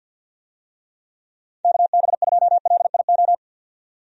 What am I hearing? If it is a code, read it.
Morse code, heard as a keyed tone, about 50 wpm, 700 Hz. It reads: KB2LIO
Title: USA Call Signs - 6 Characters - 50wpm